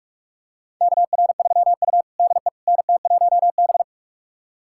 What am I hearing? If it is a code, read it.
Morse code, heard as a keyed tone, about 46 wpm, 700 Hz. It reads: KR3U DE NT1B
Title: USA Callsign DE Callsign - 4 Characters - 45wpm